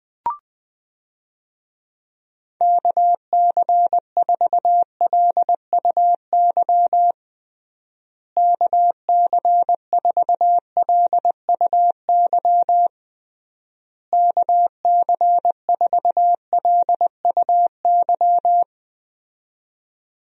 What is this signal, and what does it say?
Morse code, heard as a keyed tone, about 20 wpm, 700 Hz. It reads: KC4LUY KC4LUY KC4LUY
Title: USA Call Signs - 6 Characters - Repeated 3x - 20wpm